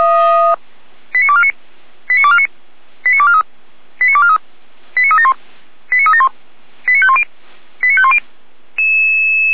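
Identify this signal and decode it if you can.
Five-tone selective call (ZVEI), heard as quick runs of five tones, tones of about 70 ms. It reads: 89249 89249 89243 89243 89371 89371 89510 89510
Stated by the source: Five-tone ZVEI selective call received on SDR, from audio.f85360.feuerwehr-orlamuende.org (Feuerwehr Orlamünde, fire brigade).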